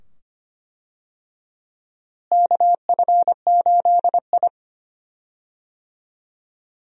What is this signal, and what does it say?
Morse code, heard as a keyed tone, about 25 wpm, 700 Hz. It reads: KF8I
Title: USA Call Signs - 4 Characters - Repeated 2x - 25wpm